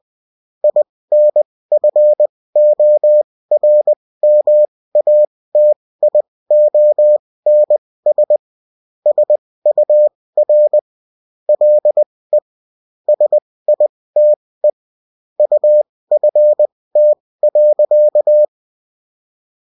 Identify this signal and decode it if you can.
Morse code, heard as a keyed tone, about 20 wpm, spaced as for 16 wpm, 600 Hz. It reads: INFORMATIONS SUR LE SITE UFT.